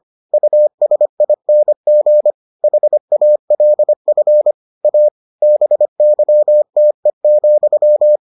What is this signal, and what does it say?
Morse code, heard as a keyed tone, about 25 wpm, 600 Hz. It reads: USING HALF A BYTE,